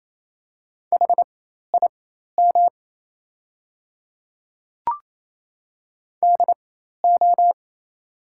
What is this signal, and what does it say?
Morse code, heard as a keyed tone, about 27 wpm, spaced as for 13 wpm, 700 Hz. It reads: HIM DO
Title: 4x Character Spacing - Top 100 Words - 28wpm